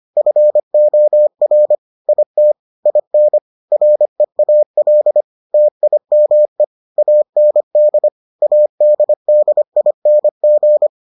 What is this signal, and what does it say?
Morse code, heard as a keyed tone, about 25 wpm, 600 Hz. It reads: FOR IT IN REAL TIME AND ADDING